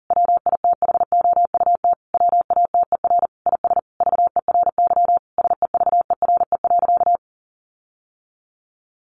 Morse code, heard as a keyed tone, about 40 wpm, 700 Hz. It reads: WITHOUT WATER IS VERY SEVERE.
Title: Sentences from Top 1500 Words 40wpm